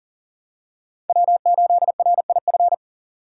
Morse code, heard as a keyed tone, about 40 wpm, 700 Hz. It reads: W8RIF
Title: USA Call Signs - 5 Characters - 3x Character Spacing - 40wpm